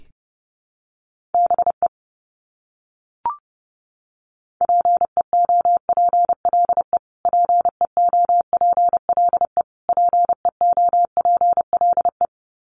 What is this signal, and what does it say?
Morse code, heard as a keyed tone, about 30 wpm, 700 Hz. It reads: BE PEOPLE PEOPLE PEOPLE
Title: Top 300 Words Repeated 30wpm